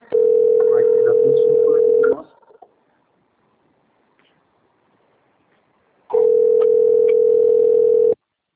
Ringback tone (North America), 440 + 480 Hz.